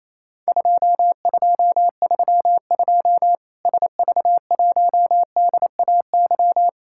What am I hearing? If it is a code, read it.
Morse code, heard as a keyed tone, about 28 wpm, 700 Hz. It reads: 2232 SV1DAY
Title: Worked All Europe DX Contest - QTC Exchange - 28wpm